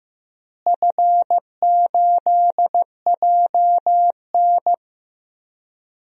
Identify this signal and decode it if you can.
Morse code, heard as a keyed tone, about 15 wpm, 700 Hz. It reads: F8JN
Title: DXCC-100 Call Signs - 4 Characters - Repeated 3x - 15wpm